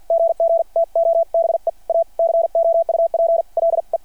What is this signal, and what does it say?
Morse code, heard as a keyed tone, about 50 wpm, spaced as for 43 wpm, 690 Hz. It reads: GG TO BE A COUPLE
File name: archived / CRSnet